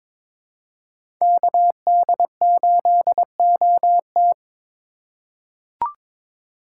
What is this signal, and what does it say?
Morse code, heard as a keyed tone, about 22 wpm, 700 Hz. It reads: KD8OT
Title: USA Call Signs - 5 Characters - 4x Character Spacing - 22wpm